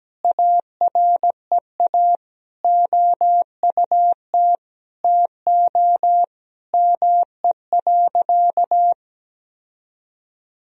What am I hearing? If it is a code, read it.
Morse code, heard as a keyed tone, about 17 wpm, 700 Hz. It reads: AREA OUT TO ME.
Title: Sentences from Top 1000 Words - Review Part 2 - 17wpm